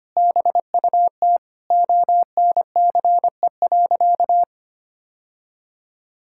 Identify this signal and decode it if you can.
Morse code, heard as a keyed tone, about 25 wpm, 700 Hz. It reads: BUT ONCE.
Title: Sentences from Top 1600 Words 25wpm